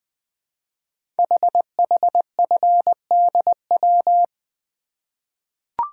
Morse code, heard as a keyed tone, about 20 wpm, 700 Hz. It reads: HHFDW
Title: Code Groups - 5 Letters - 20wpm